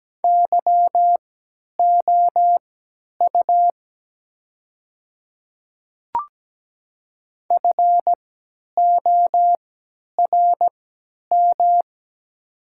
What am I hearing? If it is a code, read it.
Morse code, heard as a keyed tone, about 17 wpm, spaced as for 10 wpm, 700 Hz. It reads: YOU FORM
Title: 3x Character Spacing - Top 200 Words - 17wpm